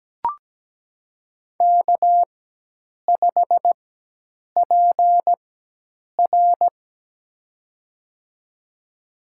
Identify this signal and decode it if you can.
Morse code, heard as a keyed tone, about 17 wpm, spaced as for 8 wpm, 700 Hz. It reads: K5PR
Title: USA Call Signs - 4 Characters - 4x Character Spacing - 17wpm